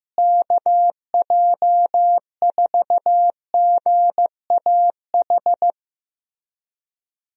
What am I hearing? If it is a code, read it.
Morse code, heard as a keyed tone, about 15 wpm, 700 Hz. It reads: KJ4GAH
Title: USA Call Signs - 6 Characters - Repeated 3x - 15wpm